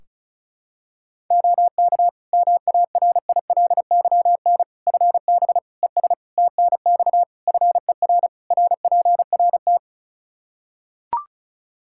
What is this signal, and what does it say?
Morse code, heard as a keyed tone, about 34 wpm, 700 Hz. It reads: OK MARILYN FB ES TNX FER RPRT
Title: Practice QSO - Exchange 2 - Info - 35wpm